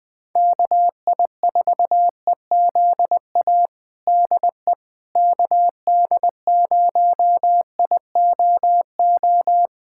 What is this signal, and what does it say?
Morse code, heard as a keyed tone, about 20 wpm, 700 Hz. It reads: KI4EZA DE KD0IOO
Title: USA Callsign DE Callsign - 6 Characters - 20wpm